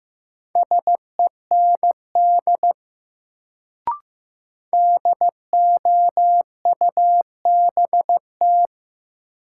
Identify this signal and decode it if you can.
Morse code, heard as a keyed tone, about 15 wpm, 700 Hz. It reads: SEND DOUBT
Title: Sentences from Top 1500 Words - New Word Drill Alt - 15wpm